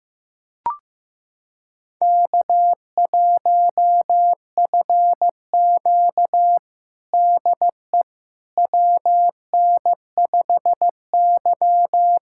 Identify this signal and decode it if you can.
Morse code, heard as a keyed tone, about 15 wpm, 700 Hz. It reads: K1FQ DE WN5Y
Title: USA Callsign DE Callsign - 4 Characters - 15wpm